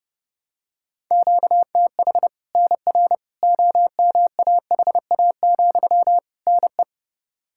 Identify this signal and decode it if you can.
Morse code, heard as a keyed tone, about 30 wpm, 700 Hz. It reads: QTH NR OMAHA, NE
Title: Practice QSO - Exchange 1 - Beginning - 30wpm